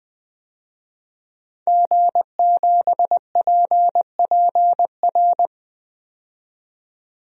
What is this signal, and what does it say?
Morse code, heard as a keyed tone, about 20 wpm, 700 Hz. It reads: G7PPR
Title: DXCC-100 Call Signs - 5 Characters - 20wpm